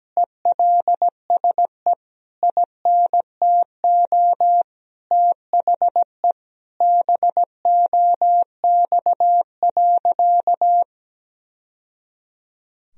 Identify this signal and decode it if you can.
Morse code, heard as a keyed tone, about 17 wpm, 700 Hz. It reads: ELSE INTO THE BOX.
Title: Sentences from Top 1000 Words - Review Part 2 - 17wpm